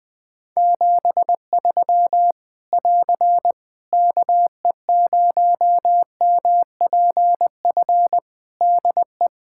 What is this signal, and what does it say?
Morse code, heard as a keyed tone, about 21 wpm, 700 Hz. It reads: 73 <AR> KE0MPF DE
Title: Practice QSO - Exchange 4 - Ending - 20wpm